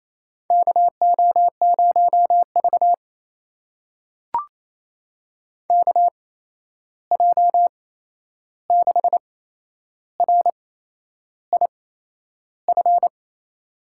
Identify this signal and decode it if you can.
Morse code, heard as a keyed tone, about 28 wpm, spaced as for 11 wpm, 700 Hz. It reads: KO0V KJ6RIF
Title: USA Call Signs - 8x Character Spacing - 28wpm